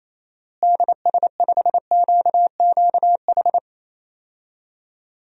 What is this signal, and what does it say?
Morse code, heard as a keyed tone, about 28 wpm, 700 Hz. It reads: DS5QQH
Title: DXCC-100 Call Signs - 6 Characters - Repeated 3x - 28wpm